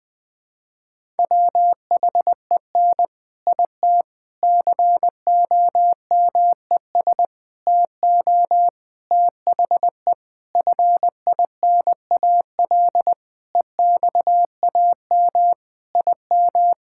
Morse code, heard as a keyed tone, about 20 wpm, 700 Hz. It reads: WHEN IT COMES TO THE FINAL EXAM IM
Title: Sentences with 5-Word Phrases - Spoken Beginning - 20wpm